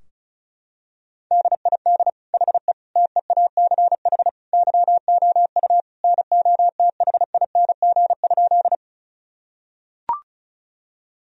Morse code, heard as a keyed tone, about 36 wpm, 700 Hz. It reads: DID HE TEACH YOU NOTHING?
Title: Sentences from Top 1000 Words - Review Part 2 - 35wpm